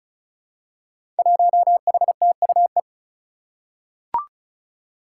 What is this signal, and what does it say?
Morse code, heard as a keyed tone, about 34 wpm, 700 Hz. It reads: 1HTUE
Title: Five Challenging Letter-Number Combinations 35wpm